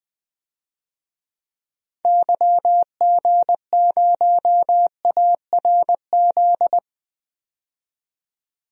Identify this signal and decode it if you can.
Morse code, heard as a keyed tone, about 20 wpm, 700 Hz. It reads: YG0ARZ
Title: DXCC-100 Call Signs - 6 Characters - 20wpm